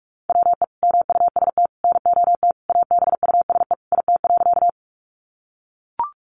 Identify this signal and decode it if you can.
Morse code, heard as a keyed tone, about 44 wpm, 700 Hz. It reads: WE MUST NOT ABUSE IT.
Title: Sentences from Top 1300 Words 45wpm